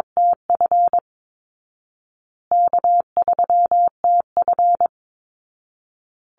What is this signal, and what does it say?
Morse code, heard as a keyed tone, about 22 wpm, 700 Hz. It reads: TF K3TF
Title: Easy Call Signs Repeated 3x - 22wpm